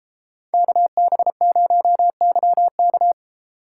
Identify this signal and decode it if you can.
Morse code, heard as a keyed tone, about 33 wpm, 700 Hz. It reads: KB0YK